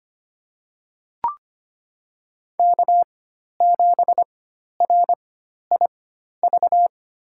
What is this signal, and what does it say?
Morse code, heard as a keyed tone, about 25 wpm, spaced as for 12 wpm, 700 Hz. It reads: K7RIV
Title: USA Call Signs - 5 Characters - 4x Character Spacing - 25wpm